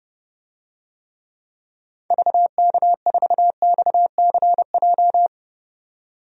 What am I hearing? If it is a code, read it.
Morse code, heard as a keyed tone, about 30 wpm, 700 Hz. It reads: VK4XCJ
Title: DXCC-100 Call Signs - 6 Characters - 30wpm